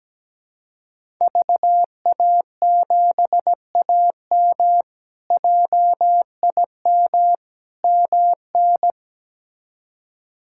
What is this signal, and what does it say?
Morse code, heard as a keyed tone, about 17 wpm, 700 Hz. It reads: VA7AM JIM MN
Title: K1USN SST QSO Practice - 17wpm